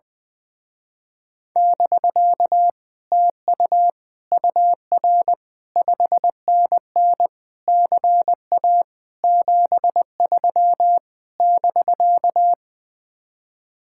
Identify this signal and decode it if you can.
Morse code, heard as a keyed tone, about 20 wpm, 700 Hz. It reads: <BK> TU UR 5NN CA 73 <BK>
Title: Speed-Racing Common QSO Phrases Encore - 20wpm